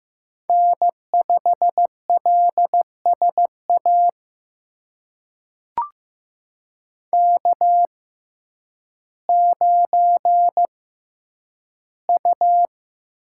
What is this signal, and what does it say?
Morse code, heard as a keyed tone, about 15 wpm, spaced as for 9 wpm, 700 Hz. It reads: N5LSA K9U
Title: USA Call Signs - 5 Characters - 6x Character Spacing - 15wpm